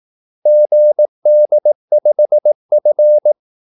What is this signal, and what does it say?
Morse code, heard as a keyed tone, about 18 wpm, 600 Hz. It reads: GD5F
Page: Kaldesignaler 18 wpm